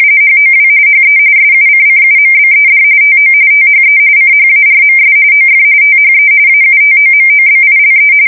RTTY, about 45.45 baud. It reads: HOW HE SUCCUMBED TO THE CHARMS OF THE AUTOMOBILE